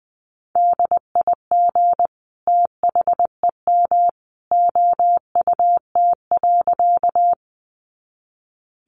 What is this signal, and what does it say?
Morse code, heard as a keyed tone, about 20 wpm, 700 Hz. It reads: DIG THEM OUT.
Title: A Sound Of Thunder 20wpm